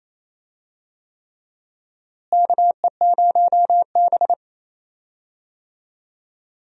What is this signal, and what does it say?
Morse code, heard as a keyed tone, about 28 wpm, 700 Hz. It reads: KE0B
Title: USA Call Signs - 4 Characters - Repeated 3x - 28wpm